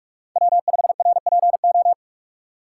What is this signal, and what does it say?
Morse code, heard as a keyed tone, about 43 wpm, 700 Hz. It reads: W5RPO
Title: USA Call Signs - 5 Characters - 2x Character Spacing - 45wpm